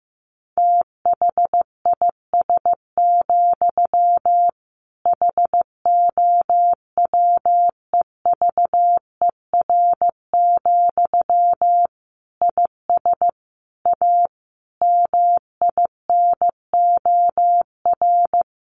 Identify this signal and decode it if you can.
Morse code, heard as a keyed tone, about 15 wpm, 700 Hz. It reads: THIS, HOWEVER, IS A MINOR
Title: Sentences from Top 1500 Words 15wpm